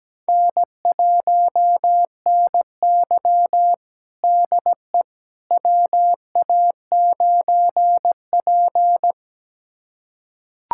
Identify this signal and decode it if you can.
Morse code, heard as a keyed tone, about 17 wpm, 700 Hz. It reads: N1NY DE WA9P
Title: USA Callsign DE Callsign - 4 Characters - 17wpm